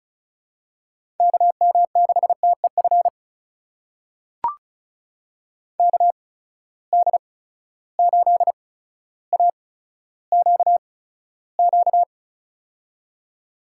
Morse code, heard as a keyed tone, about 36 wpm, spaced as for 15 wpm, 700 Hz. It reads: KM6TEF KD8AQQ
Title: USA Call Signs - 6 Characters - 8x Character Spacing - 35wpm